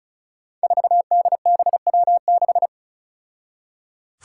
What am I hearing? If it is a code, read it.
Morse code, heard as a keyed tone, about 36 wpm, 700 Hz. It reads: 4DBW6
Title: Five Challenging Letter-Number Combinations 35wpm